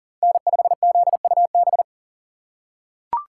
Morse code, heard as a keyed tone, about 39 wpm, 700 Hz. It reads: N5ZUB
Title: USA Call Signs - 4x Character Spacing - 40wpm